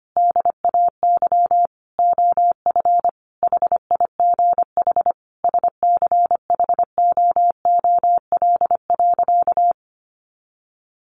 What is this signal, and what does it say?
Morse code, heard as a keyed tone, about 25 wpm, 700 Hz. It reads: DAY OF HIGH SCHOOL.
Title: Additional Easy Phrases 25wpm